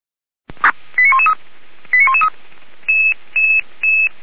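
Five-tone selective call (ZVEI), heard as quick runs of five tones, tones of about 70 ms. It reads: 89113 89113
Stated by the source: ZVEI five-tone selective call received by SDR, from the Feuerwehr Orlamünde audio server.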